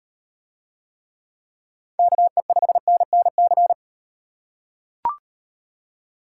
Morse code, heard as a keyed tone, about 38 wpm, 700 Hz. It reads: KE5NNC